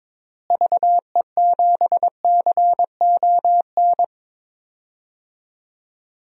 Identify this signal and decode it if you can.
Morse code, heard as a keyed tone, about 22 wpm, 700 Hz. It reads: VE7CON
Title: DXCC-100 Call Signs - 6 Characters - Repeated 2x - 22wpm